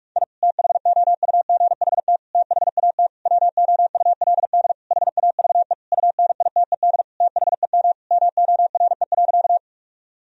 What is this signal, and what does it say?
Morse code, heard as a keyed tone, about 41 wpm, 700 Hz. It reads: I THOUGHT THAT WOULD HAVE UNITED THEM MORE.
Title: Sentences from Top 1000 Words - Review Part 4 - 45wpm